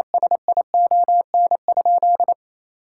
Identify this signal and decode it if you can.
Morse code, heard as a keyed tone, about 28 wpm, 700 Hz. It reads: SION?
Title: Sentences from Top 1000 Words - Review Part 5 - 28wpm